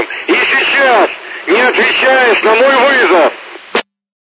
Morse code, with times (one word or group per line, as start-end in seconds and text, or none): none